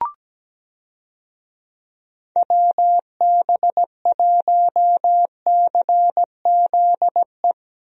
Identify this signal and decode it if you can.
Morse code, heard as a keyed tone, about 17 wpm, 700 Hz. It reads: WB1CZE
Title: USA Call Signs - 6 Characters - Repeated 3x - 17wpm